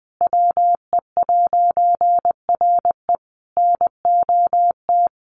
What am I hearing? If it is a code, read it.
Morse code, heard as a keyed tone, about 20 wpm, 700 Hz. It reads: WE'RE NOT
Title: A Sound Of Thunder 20wpm